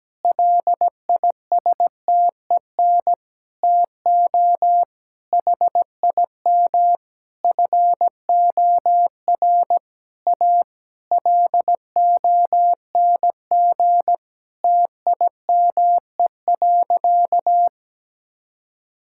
Morse code, heard as a keyed tone, about 17 wpm, 700 Hz. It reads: LISTEN TO HIM FOR A LONG TIME.
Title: Sentences from Top 1600 Words 17wpm